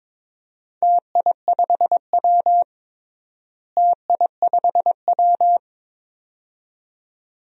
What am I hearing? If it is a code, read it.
Morse code, heard as a keyed tone, about 22 wpm, 700 Hz. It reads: TI5W TI5W
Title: DXCC-100 Call Signs - 4 Characters - Repeated 2x - 22wpm